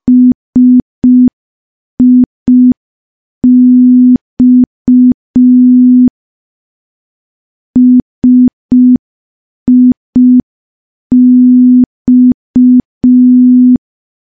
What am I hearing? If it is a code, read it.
Morse code, heard as a keyed tone, about 5 wpm, 250 Hz. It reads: SIX SIX
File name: morse / morse-training-05WPM